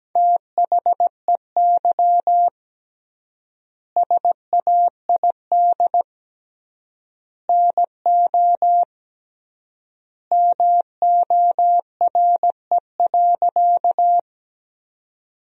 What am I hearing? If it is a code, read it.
Morse code, heard as a keyed tone, about 17 wpm, 700 Hz. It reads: THEY SAID NO MORE.
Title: 3x Word-Spacing - Sentences from Top 100 Words - 17wpm